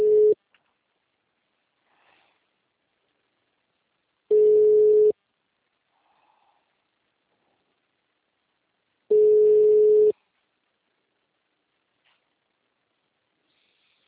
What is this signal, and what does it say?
Ringback tone (Europe, 425 Hz).
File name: out- -2016- - - .0